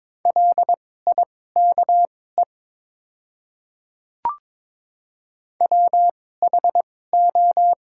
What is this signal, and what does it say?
Morse code, heard as a keyed tone, about 22 wpm, spaced as for 16 wpm, 700 Hz. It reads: LIKE WHO
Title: 2x Character Spacing - Top 100 Words - 22wpm